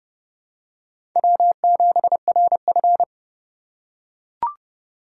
Morse code, heard as a keyed tone, about 30 wpm, 700 Hz. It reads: W7RF
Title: USA Call Signs - 3x Character Spacing - 30wpm